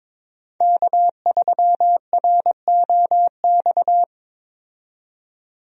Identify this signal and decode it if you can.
Morse code, heard as a keyed tone, about 22 wpm, 700 Hz. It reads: K3ROX
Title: USA Call Signs - 5 Characters - 22wpm